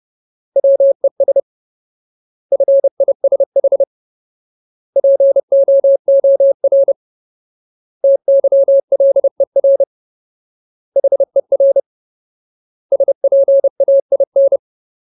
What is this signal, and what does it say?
Morse code, heard as a keyed tone, about 30 wpm, 550 Hz. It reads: WES FISH POOR TYLER HER SPAIN
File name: Ultimate Word List / 30WPM Ultimate